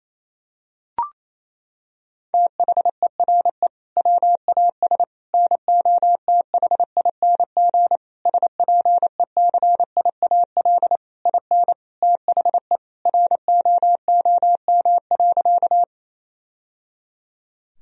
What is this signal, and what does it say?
Morse code, heard as a keyed tone, about 28 wpm, 700 Hz. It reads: THERE WAS NOTHING SPECIAL IN THE ROOM.
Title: Sentences from Top 1000 Words - Review Part 3 - 28wpm